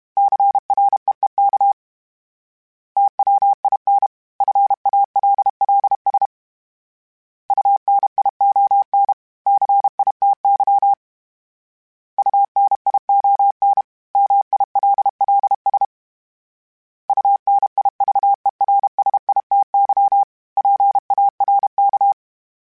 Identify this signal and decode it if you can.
Morse code, heard as a keyed tone, about 32 wpm, 800 Hz. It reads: CREEK TWIN FALLS UNION CITY UNION MILLS UNIVERSITY PARK
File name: fromSMs / FromSMs mp3s / Cities@30